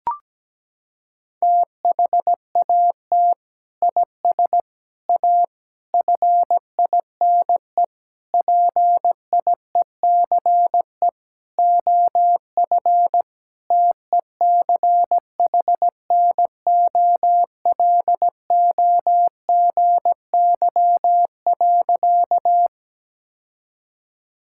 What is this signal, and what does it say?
Morse code, heard as a keyed tone, about 17 wpm, 700 Hz. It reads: THAT IS A FINE PIECE OF TECHNOLOGY.
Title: Sentences from Top 1000 Words - Review Part 5 - 17wpm